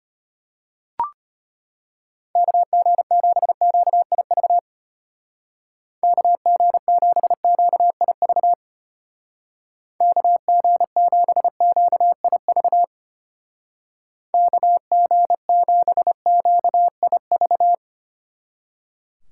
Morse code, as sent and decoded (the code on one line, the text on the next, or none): -.- --. --... --.- .. ...- / -.- --. --... --.- .. ...- / -.- --. --... --.- .. ...- / -.- --. --... --.- .. ...-
KG7QIV KG7QIV KG7QIV KG7QIV